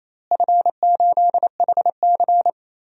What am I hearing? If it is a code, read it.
Morse code, heard as a keyed tone, about 28 wpm, 700 Hz. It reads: F8HC
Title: DXCC-100 Call Signs - 4 Characters - Repeated 3x - 28wpm